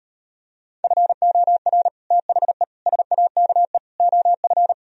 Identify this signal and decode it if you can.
Morse code, heard as a keyed tone, about 38 wpm, 700 Hz. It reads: FOR THE SAKE OF